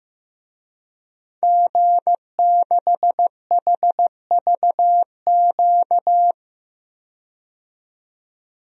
Morse code, heard as a keyed tone, about 15 wpm, 700 Hz. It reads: G6HVQ
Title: DXCC-100 Call Signs - 5 Characters - 15wpm